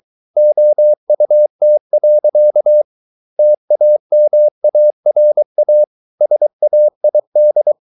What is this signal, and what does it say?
Morse code, heard as a keyed tone, about 23 wpm, 600 Hz. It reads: OUT. TAMARA SAID